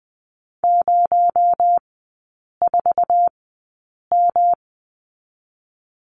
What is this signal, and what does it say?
Morse code, heard as a keyed tone, about 20 wpm, 700 Hz. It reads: 0 4 M
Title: 218 Lesson 35e - Number 0 - Three Characters 20wpm